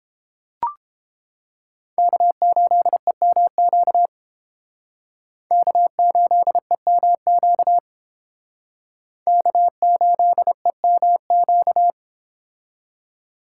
Morse code, heard as a keyed tone, about 30 wpm, 700 Hz. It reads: K8EMQ K8EMQ K8EMQ